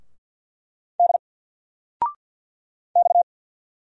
Morse code, heard as a keyed tone, about 50 wpm, 700 Hz. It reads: D X
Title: Single Letter-Number 50wpm